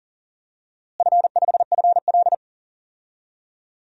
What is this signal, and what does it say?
Morse code, heard as a keyed tone, about 37 wpm, 700 Hz. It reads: F5FL
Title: DXCC-100 Call Signs - 4 Characters - Repeated 3x - 40wpm